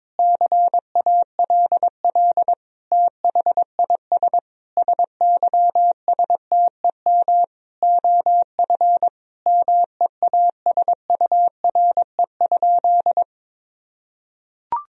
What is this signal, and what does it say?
Morse code, heard as a keyed tone, about 22 wpm, 700 Hz. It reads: CALL THIS SYSTEM OF MEASURE?
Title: Sentences from Top 1000 Words - Review Part 3 - 22wpm